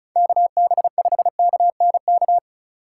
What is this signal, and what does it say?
Morse code, heard as a keyed tone, about 36 wpm, 700 Hz. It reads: KB5KNK
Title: USA Call Signs - 6 Characters - 2x Character Spacing - 35wpm